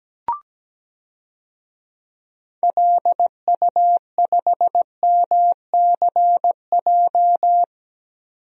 Morse code, heard as a keyed tone, about 17 wpm, 700 Hz. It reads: LU5MCJ
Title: DXCC-100 Call Signs - 6 Characters - 17wpm